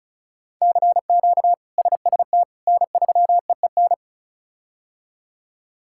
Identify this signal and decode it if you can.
Morse code, heard as a keyed tone, about 34 wpm, 700 Hz. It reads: CQ SST N3EEN
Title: K1USN SST QSO Practice - 35wpm